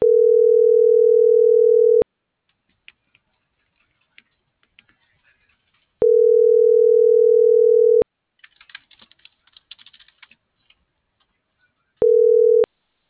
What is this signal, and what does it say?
Ringback tone (North America), 440 + 480 Hz.